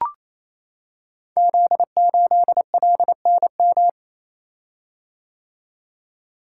Morse code, heard as a keyed tone, about 27 wpm, 700 Hz. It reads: Z8LNM
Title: Code Groups - 5 Letter:Numbers - 28wpm